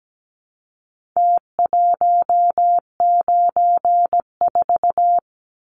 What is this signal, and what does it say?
Morse code, heard as a keyed tone, about 17 wpm, 700 Hz. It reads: T194
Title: Numbers - Contest Style - 4-Digit 17wpm